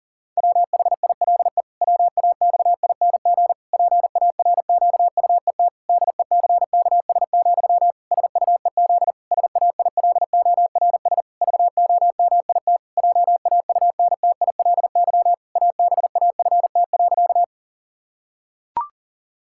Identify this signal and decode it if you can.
Morse code, heard as a keyed tone, about 40 wpm, 700 Hz. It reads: WHILE WAXING PARQUET DECKS, SUEZ SAILORS VOMIT JAUNTILY ABAFT.
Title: Pangram Challenge v2 - 40wpm